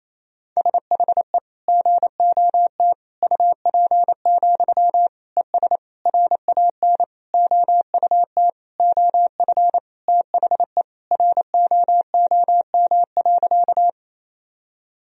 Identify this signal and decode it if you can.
Morse code, heard as a keyed tone, about 28 wpm, 700 Hz. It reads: SHE GOT UP, ES RAN OUT OF THE ROOM.
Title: Sentences from Top 500 Words and Abbreviations 28wpm